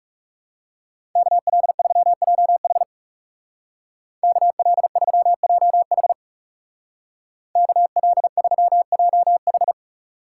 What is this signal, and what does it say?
Morse code, heard as a keyed tone, about 36 wpm, 700 Hz. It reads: KL3JH KL3JH KL3JH
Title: Speed Racing - USA Call Signs 30wpm